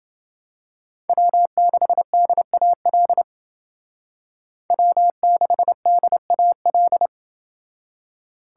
Morse code, heard as a keyed tone, about 28 wpm, 700 Hz. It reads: W6DAL W6DAL